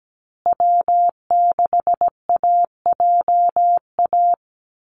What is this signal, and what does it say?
Morse code, heard as a keyed tone, about 17 wpm, 700 Hz. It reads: W6AJA
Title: USA Call Signs Repeated 2x 17wpm